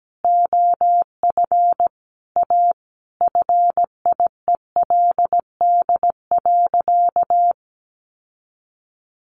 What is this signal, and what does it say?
Morse code, heard as a keyed tone, about 17 wpm, 700 Hz. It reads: OF A FIELD.
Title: Sentences from Top 1000 Words 17wpm